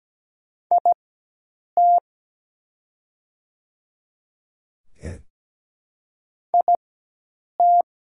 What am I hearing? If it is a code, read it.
Morse code, heard as a keyed tone, about 17 wpm, spaced as for 8 wpm, 690 Hz. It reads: IT IT
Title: 4x Character Spacing - Top 200 Words - 17wpm